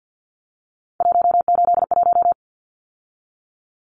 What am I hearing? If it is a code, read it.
Morse code, heard as a keyed tone, about 50 wpm, 700 Hz. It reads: 181
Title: Numbers - 3-Digit 50wpm